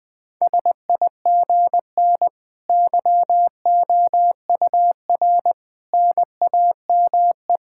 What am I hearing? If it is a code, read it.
Morse code, heard as a keyed tone, about 20 wpm, 700 Hz. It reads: SIGN YOUR NAME
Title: Sentences with 5-Word Phrases - 20wpm